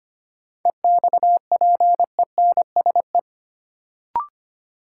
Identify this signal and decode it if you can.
Morse code, heard as a keyed tone, about 25 wpm, 700 Hz. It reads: EXPENSE
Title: Sets of 4 Words - New Word Drill - 25wpm